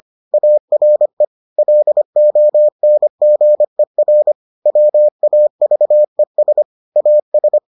Morse code, heard as a keyed tone, about 25 wpm, 600 Hz. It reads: ARE LONGER WAVES AS